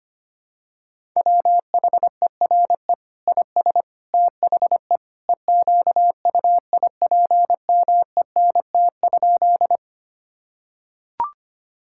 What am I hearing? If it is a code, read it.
Morse code, heard as a keyed tone, about 25 wpm, 700 Hz. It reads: WHERE IS THE EQUIPMENT?
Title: Sentences from Top 1000 Words - Review Part 5 - 25wpm